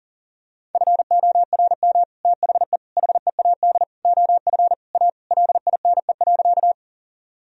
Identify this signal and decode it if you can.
Morse code, heard as a keyed tone, about 40 wpm, 700 Hz. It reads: FORM THE HEAD OF A LINE.
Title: Sentences from Top 1000 Words - Review Part 1 - 40wpm